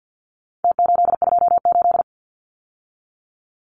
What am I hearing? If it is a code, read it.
Morse code, heard as a keyed tone, about 48 wpm, 700 Hz. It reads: T828
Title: Numbers - Contest Style - 4-Digit 50wpm